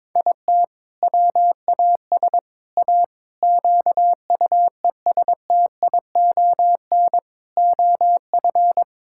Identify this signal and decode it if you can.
Morse code, heard as a keyed tone, about 21 wpm, 700 Hz. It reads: IT WAS A QUESTION OF
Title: Sentences from Top 1000 Words - Review Part 6 - 22wpm